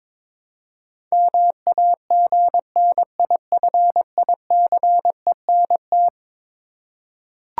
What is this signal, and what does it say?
Morse code, heard as a keyed tone, about 22 wpm, 700 Hz. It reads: MAGNIFICENT
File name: practice / 22wpm / Eleven Letter Words - 22wpm